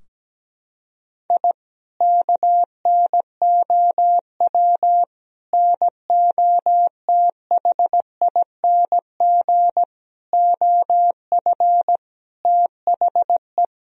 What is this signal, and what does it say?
Morse code, heard as a keyed tone, about 17 wpm, 700 Hz. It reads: I KNOW NOTHING OF THE
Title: Sentences from Top 1000 Words - Review Part 3 - 17wpm